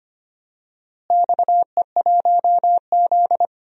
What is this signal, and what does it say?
Morse code, heard as a keyed tone, about 25 wpm, 700 Hz. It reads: XE1Z
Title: DXCC-100 Call Signs - 4 Characters - Repeated 3x - 25wpm